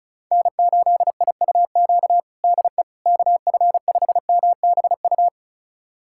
Morse code, heard as a keyed tone, about 36 wpm, 700 Hz. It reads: N8IUQ DE KF5MBU
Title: USA Callsign DE Callsign 35wpm